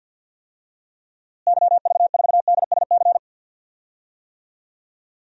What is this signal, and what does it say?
Morse code, heard as a keyed tone, about 50 wpm, 700 Hz. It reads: YV4DSC
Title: DXCC-100 Call Signs - 6 Characters - Repeated 2x - 50wpm